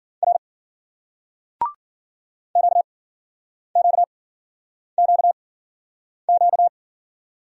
Morse code, heard as a keyed tone, about 50 wpm, spaced as for 9 wpm, 700 Hz. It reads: R QQQQ